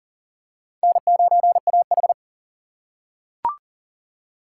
Morse code, heard as a keyed tone, about 40 wpm, 700 Hz. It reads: N9AH
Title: USA Call Signs - 4 Characters - 4x Character Spacing - 40wpm